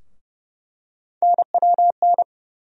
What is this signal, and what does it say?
Morse code, heard as a keyed tone, about 30 wpm, 700 Hz. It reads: NWN
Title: USA Call Sign Postfixes - 3 Letters 30wpm